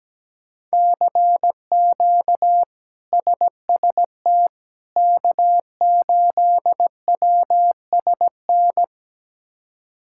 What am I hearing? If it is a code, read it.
Morse code, heard as a keyed tone, about 17 wpm, 700 Hz. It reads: CQ SST K8WSN
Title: K1USN SST QSO Practice - 17wpm